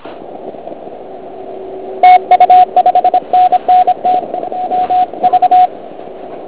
Morse code, heard as a keyed tone, about 26 wpm, 730 Hz, with radio noise. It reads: TU 5NN T MV
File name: Archiv / CUC106 / cuc106 PILEUP